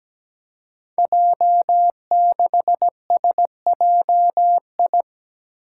Morse code, heard as a keyed tone, about 17 wpm, 700 Hz. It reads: J6SJI
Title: Five Challenging Letter-Number Combinations 17wpm